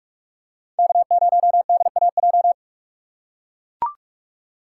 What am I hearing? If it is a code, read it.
Morse code, heard as a keyed tone, about 44 wpm, 700 Hz. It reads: K0DAJ